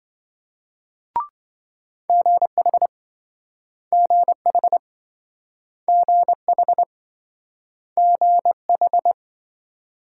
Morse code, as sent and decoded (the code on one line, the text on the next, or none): --. .... / --. .... / --. .... / --. ....
GH GH GH GH